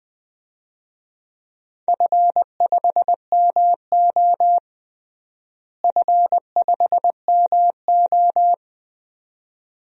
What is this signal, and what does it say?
Morse code, heard as a keyed tone, about 20 wpm, 700 Hz. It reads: F5MO F5MO
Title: DXCC-100 Call Signs - 4 Characters - Repeated 2x - 20wpm